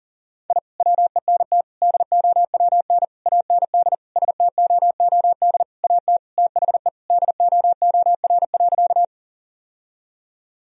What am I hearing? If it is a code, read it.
Morse code, heard as a keyed tone, about 40 wpm, 700 Hz. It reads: I WENT DOWN AND STOOD AT THE DOOR.
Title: Sentences from Top 1000 Words - Review Part 2 - 40wpm